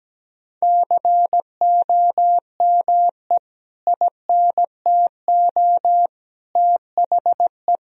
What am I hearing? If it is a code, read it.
Morse code, heard as a keyed tone, about 17 wpm, 700 Hz. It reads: COME INTO THE
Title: Sentences from Top 1000 Words - Review Part 4 - 17wpm